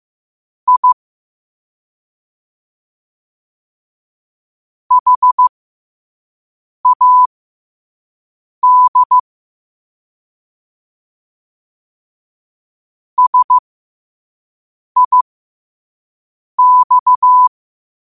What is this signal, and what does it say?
Morse code, heard as a keyed tone, about 15 wpm, spaced as for 5 wpm, 1000 Hz. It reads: I HAD SIX
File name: qotdfiles / 5wpm